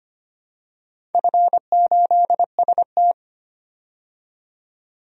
Morse code, heard as a keyed tone, about 25 wpm, 700 Hz. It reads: F8ST
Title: DXCC-100 Call Signs - 4 Characters - 25wpm